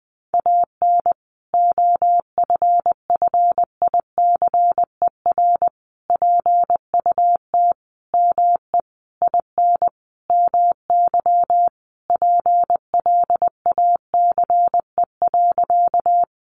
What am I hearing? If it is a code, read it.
Morse code, heard as a keyed tone, about 20 wpm, 700 Hz. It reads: AN OFFICER PUT ME IN MY PLACE.
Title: Sentences from Top 1200 Words 20wpm